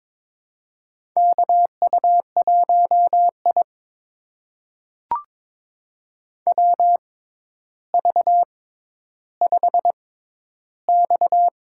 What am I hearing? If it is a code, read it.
Morse code, heard as a keyed tone, about 22 wpm, spaced as for 11 wpm, 700 Hz. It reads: KU1I WV5X
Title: USA Call Signs - 6x Character Spacing - 22wpm